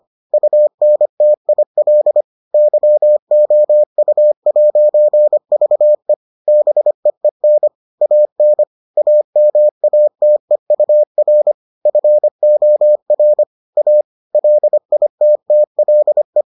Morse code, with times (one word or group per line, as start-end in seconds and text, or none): 0.3-2.2 UNTIL
2.5-6.1 YOU'VE
6.5-7.7 BEEN
8.0-8.6 AN
9.0-11.5 AMATEUR
11.9-13.4 FOR
13.8-14.0 A
14.3-16.4 LITTLE